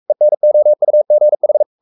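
Morse code, heard as a keyed tone, about 44 wpm, 600 Hz. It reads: ENOUGH
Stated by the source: Morse code at 43 wpm sending Ken Follett's novel Never in English, from CW med Gnister.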